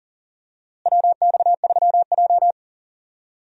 Morse code, heard as a keyed tone, about 40 wpm, 700 Hz. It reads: WX3J
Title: USA Call Signs - 4 Characters - Repeated 2x - 40wpm